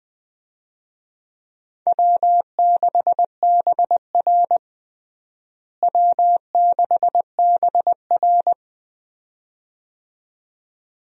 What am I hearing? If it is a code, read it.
Morse code, heard as a keyed tone, about 20 wpm, 700 Hz. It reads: W6BR W6BR
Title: USA Call Signs - 4 Characters - Repeated 2x - 20wpm